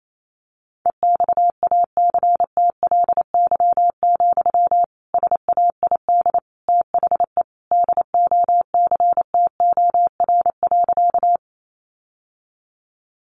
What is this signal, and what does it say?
Morse code, heard as a keyed tone, about 28 wpm, 700 Hz. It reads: EXACTLY, SAID THE DOCTOR.
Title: Sentences from Top 1500 Words 28wpm